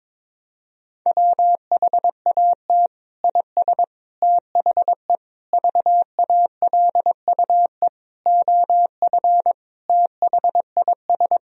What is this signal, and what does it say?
Morse code, heard as a keyed tone, about 22 wpm, 700 Hz. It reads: WHAT IS THE VALUE OF THIS
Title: Sentences from Top 1000 Words - Review Part 3 - 22wpm